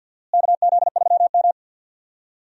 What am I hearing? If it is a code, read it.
Morse code, heard as a keyed tone, about 47 wpm, 700 Hz. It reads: KZ3M
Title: USA Call Signs - 4 Characters - Repeated 2x - 50wpm